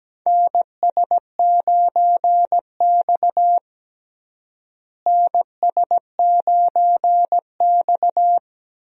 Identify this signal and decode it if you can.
Morse code, heard as a keyed tone, about 17 wpm, 700 Hz. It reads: NS9X NS9X
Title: USA Call Signs - 4 Characters - Repeated 2x - 17wpm